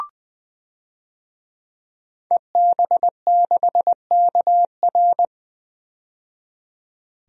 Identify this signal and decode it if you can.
Morse code, heard as a keyed tone, about 20 wpm, 700 Hz. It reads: EB6KR
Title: DXCC-100 Call Signs - 5 Characters - 20wpm